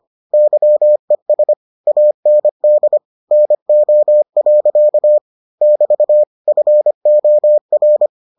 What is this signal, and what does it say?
Morse code, heard as a keyed tone, about 25 wpm, 600 Hz. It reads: YES AND NO. <BT> FOR